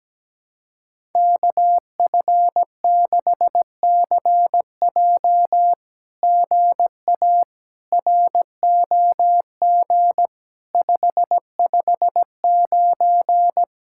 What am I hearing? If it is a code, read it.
Morse code, heard as a keyed tone, about 17 wpm, 700 Hz. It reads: KF6CJ GA ROG 559
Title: SOTA QSO Practice 17wpm